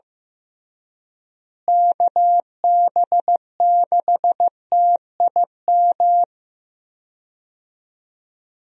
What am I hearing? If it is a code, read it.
Morse code, heard as a keyed tone, about 15 wpm, 700 Hz. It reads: KB6TIM
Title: USA Call Signs - 6 Characters - Repeated 3x - 15wpm